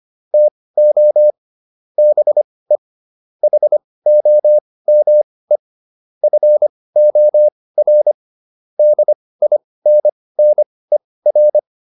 Morse code, heard as a keyed tone, about 25 wpm, spaced as for 18 wpm, 600 Hz. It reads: TO BE HOME FOR DINNER